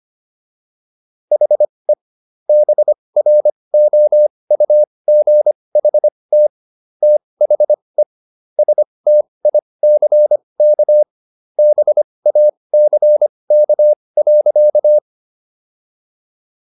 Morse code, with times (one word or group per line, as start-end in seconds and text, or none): 1.3-1.9 HE
2.5-6.5 BROUGHT
7.0-8.0 THE
8.6-11.0 STICK
11.6-15.0 BACK.